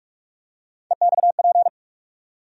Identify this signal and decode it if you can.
Morse code, heard as a keyed tone, about 42 wpm, 700 Hz. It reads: EXP
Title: USA Call Sign Postfixes - 3 Letters 45wpm